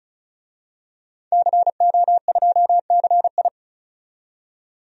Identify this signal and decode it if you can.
Morse code, heard as a keyed tone, about 36 wpm, 700 Hz. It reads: CO2CI
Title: DXCC-100 Call Signs - 5 Characters - 35wpm